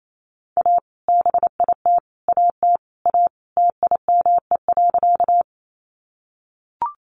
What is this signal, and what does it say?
Morse code, heard as a keyed tone, about 28 wpm, 700 Hz. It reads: A BIT AT A TIME.
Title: Sentences from Top 900 Words 28wpm